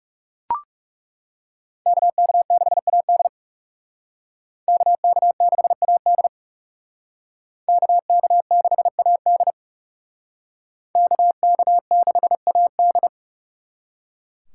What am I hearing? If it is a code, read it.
Morse code, heard as a keyed tone, about 36 wpm, 700 Hz. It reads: KK6AD KK6AD KK6AD KK6AD
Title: Speed Racing - USA Call Signs 30wpm